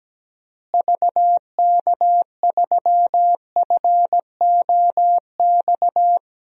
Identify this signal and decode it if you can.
Morse code, heard as a keyed tone, about 17 wpm, 700 Hz. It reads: VK3FOX
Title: DXCC-100 Call Signs - 6 Characters - Repeated 3x - 17wpm